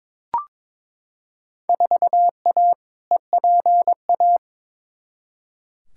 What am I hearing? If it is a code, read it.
Morse code, heard as a keyed tone, about 22 wpm, 700 Hz. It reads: 4A EPA
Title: Field Day Exchanges - 22wpm